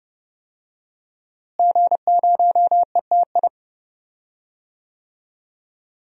Morse code, heard as a keyed tone, about 32 wpm, 700 Hz. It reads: G0ETI
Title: DXCC-100 Call Signs - 5 Characters - Repeated 3x - 30wpm